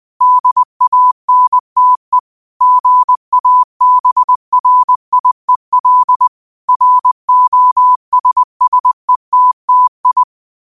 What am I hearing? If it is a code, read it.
Morse code, heard as a keyed tone, about 20 wpm, 1000 Hz. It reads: DANTE GABRIEL ROSSETTI
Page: Quotes for Fri, 15 Aug 2025 in Morse Code at 20 words per minute.